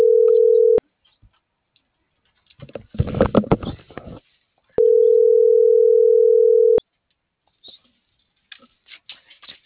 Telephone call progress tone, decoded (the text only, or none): ringback tone